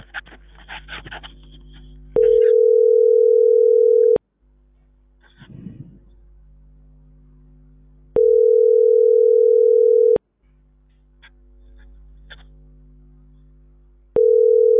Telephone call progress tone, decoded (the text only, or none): ringback tone